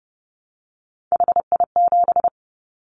Morse code, heard as a keyed tone, about 30 wpm, 700 Hz. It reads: HI7
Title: Callsign Prefixes for Top 100 DXCC - 30wpm